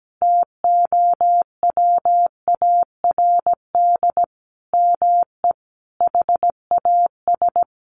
Morse code, heard as a keyed tone, about 17 wpm, 700 Hz. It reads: TOWARD ME HAS
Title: Sentences from Top 1300 Words 17wpm